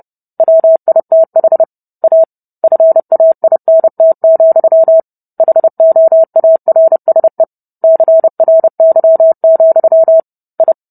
Morse code, heard as a keyed tone, about 30 wpm, 650 Hz. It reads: WITH A FAINT, HOARSE CRY, I